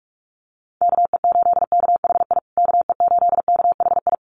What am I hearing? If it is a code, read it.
Morse code, heard as a keyed tone, about 44 wpm, 700 Hz. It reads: KE8KHI KE8KHI
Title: USA Call Signs Repeated 2x 45wpm